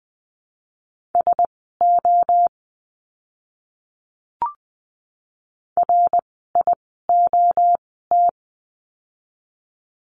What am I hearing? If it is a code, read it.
Morse code, heard as a keyed tone, about 20 wpm, spaced as for 14 wpm, 700 Hz. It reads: SO RIOT
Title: 061 Lesson 11f - Letter h - Words 20wpm